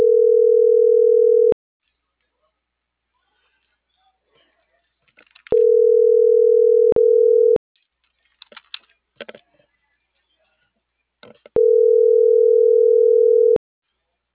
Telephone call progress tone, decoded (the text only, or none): ringback tone